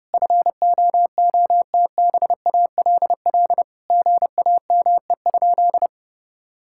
Morse code, heard as a keyed tone, about 32 wpm, 700 Hz. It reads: FOOTBALL GAME?
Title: Sentences from Top 1600 Words 30wpm